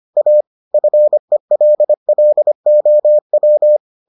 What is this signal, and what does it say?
Morse code, heard as a keyed tone, about 25 wpm, 600 Hz. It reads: A FELLOW